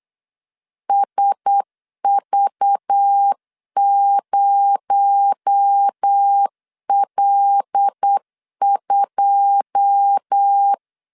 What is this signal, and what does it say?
Morse code, heard as a keyed tone, about 8 wpm, 800 Hz. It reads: SV0L2
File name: LEZIONI / 9 2 8wpm 3437